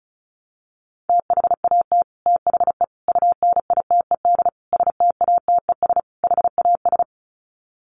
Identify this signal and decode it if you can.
Morse code, heard as a keyed tone, about 34 wpm, 700 Hz. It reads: THAT THE UNITED STATES HAS
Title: Sets of 5 Words Encore 35wpm